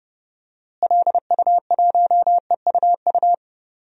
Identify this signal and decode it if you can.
Morse code, heard as a keyed tone, about 30 wpm, 700 Hz. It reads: LU1EUU
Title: DXCC-100 Call Signs - 6 Characters - Repeated 2x - 30wpm